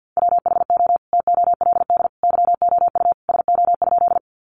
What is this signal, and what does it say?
Morse code, heard as a keyed tone, about 49 wpm, 700 Hz. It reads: WHO TOLD YOU SO?
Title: Sentences from Top 400 Words 50wpm